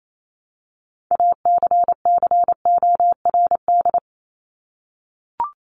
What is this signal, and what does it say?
Morse code, heard as a keyed tone, about 28 wpm, 700 Hz. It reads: ACCORD
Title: ING Suffix - Top 100 Words with Base Words 28wpm